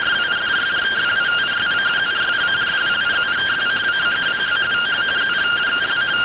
Two-tone teleprinter signal, about 170 Hz shift.